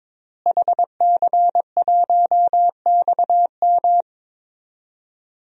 Morse code, heard as a keyed tone, about 22 wpm, 700 Hz. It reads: HC1XM
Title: DXCC-100 Call Signs - 5 Characters - Repeated 3x - 22wpm